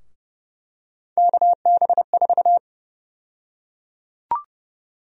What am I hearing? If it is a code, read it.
Morse code, heard as a keyed tone, about 30 wpm, 700 Hz. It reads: KB4
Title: USA Call Sign Prefixes - 30wpm